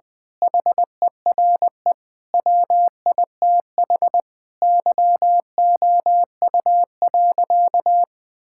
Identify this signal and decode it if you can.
Morse code, heard as a keyed tone, about 21 wpm, 700 Hz. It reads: HERE WITH YOU.
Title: Sentences from Top 1600 Words 20wpm